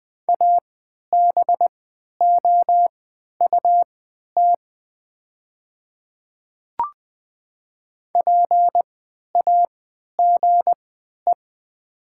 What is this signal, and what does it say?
Morse code, heard as a keyed tone, about 20 wpm, spaced as for 11 wpm, 700 Hz. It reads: ABOUT PAGE
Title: 3x Character Spacing - Top 300 Words - 20wpm